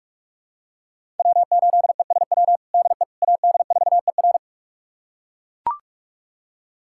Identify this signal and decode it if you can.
Morse code, heard as a keyed tone, about 41 wpm, 700 Hz. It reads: W8ESW DE AD4ER
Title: USA Callsign DE Callsign - 5 Characters - 45wpm